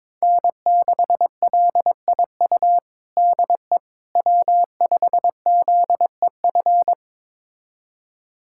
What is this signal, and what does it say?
Morse code, heard as a keyed tone, about 22 wpm, 700 Hz. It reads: N6LIU DE W5ZEF
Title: USA Callsign DE Callsign - 5 Characters - 22wpm